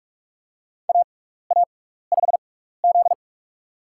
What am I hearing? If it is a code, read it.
Morse code, heard as a keyed tone, about 46 wpm, spaced as for 16 wpm, 700 Hz. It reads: AA5Z
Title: USA Call Signs - 4 Characters - 6x Character Spacing - 45wpm